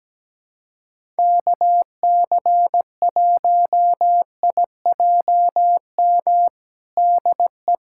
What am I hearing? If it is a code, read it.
Morse code, heard as a keyed tone, about 17 wpm, 700 Hz. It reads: KC1IJM DE
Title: USA Callsign DE Callsign - 6 Characters - 17wpm